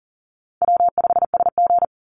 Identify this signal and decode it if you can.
Morse code, heard as a keyed tone, about 39 wpm, 700 Hz. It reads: W5SG
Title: Call Signs - 40wpm